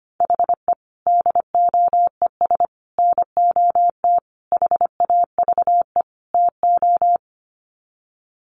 Morse code, heard as a keyed tone, about 25 wpm, 700 Hz. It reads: HE DOES NOT HAVE TO
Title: Sets of 5 Words Encore 25wpm